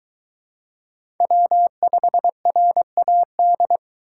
Morse code, heard as a keyed tone, about 21 wpm, 700 Hz. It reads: W5RAD
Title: Speed Racing - USA 1x3 Call Signs 17wpm